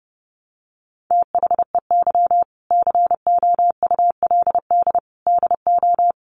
Morse code, heard as a keyed tone, about 30 wpm, 700 Hz. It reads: THEY COULD DO
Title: Sentences from Top 900 Words 30wpm